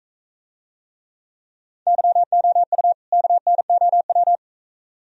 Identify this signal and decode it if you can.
Morse code, heard as a keyed tone, about 42 wpm, 700 Hz. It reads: YOU KNOW